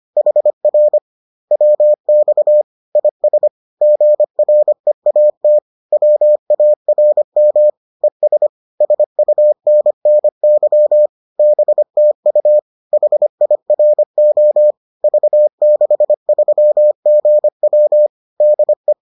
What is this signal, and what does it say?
Morse code, heard as a keyed tone, about 25 wpm, 600 Hz. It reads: HR WX IS GREAT WARM ES SUNNY BTU HIRO V63GW DE